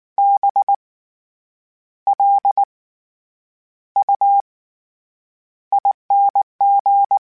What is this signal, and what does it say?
Morse code, heard as a keyed tone, about 19 wpm, spaced as for 8 wpm, 800 Hz. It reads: BLUING
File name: Special INGs / Special INGs-mp3s / SlowToFaster ing @18